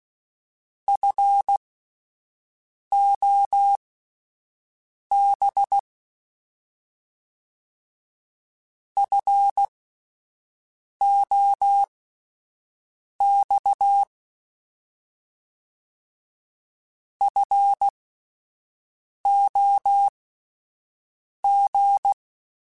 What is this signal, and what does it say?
Morse code, heard as a keyed tone, about 16 wpm, spaced as for 5 wpm, 800 Hz. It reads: FOB FOX FOG
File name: Alphabetical / Alphabetical mp3s / Fs @5